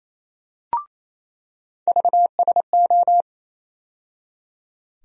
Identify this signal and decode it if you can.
Morse code, heard as a keyed tone, about 28 wpm, 700 Hz. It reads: VSO
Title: USA Call Sign Postfixes - 3 Letters 28wpm